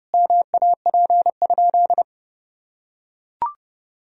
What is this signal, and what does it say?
Morse code, heard as a keyed tone, about 30 wpm, 700 Hz. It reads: MAP?
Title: Sentences from Top 1000 Words - Review Part 2 - 30wpm